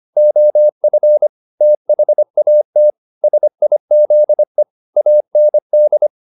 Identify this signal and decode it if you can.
Morse code, heard as a keyed tone, about 25 wpm, 600 Hz. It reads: OF THAT SIZE AND
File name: the old man and the sea part 1 25wpm